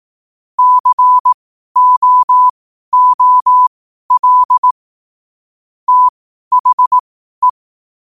Morse code, heard as a keyed tone, about 18 wpm, spaced as for 12 wpm, 1000 Hz. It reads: COOL THE
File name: qotdfiles / 12wpm